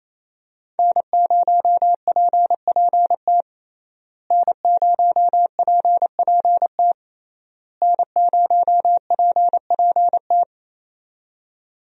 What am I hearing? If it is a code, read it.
Morse code, heard as a keyed tone, about 28 wpm, 700 Hz. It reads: N0PPT N0PPT N0PPT
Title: USA Call Signs - 5 Characters - Repeated 3x - 28wpm